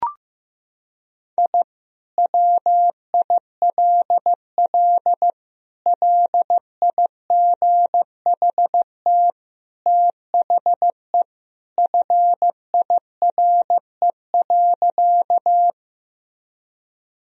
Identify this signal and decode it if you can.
Morse code, heard as a keyed tone, about 15 wpm, 700 Hz. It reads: I WILL LIGHT THE FIRE.
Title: Sentences from Top 1000 Words - Review Part 1 - 15wpm